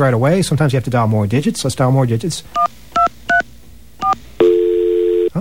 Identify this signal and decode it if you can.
DTMF, heard as telephone touch tones, tones of about 120 ms, with speech. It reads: 1234